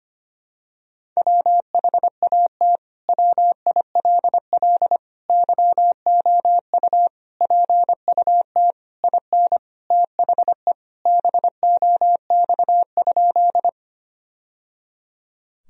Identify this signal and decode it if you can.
Morse code, heard as a keyed tone, about 25 wpm, 700 Hz. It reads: WHAT WILL YOU PUT IN THE BOX?
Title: Sentences from Top 1000 Words - Review Part 2 - 25wpm